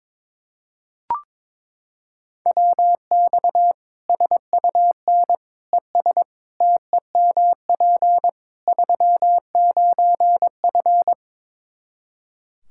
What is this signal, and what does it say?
Morse code, heard as a keyed tone, about 22 wpm, 700 Hz. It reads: WX SUN ES TEMP 39F
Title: Practice QSO - Exchange 2 - Info - 22wpm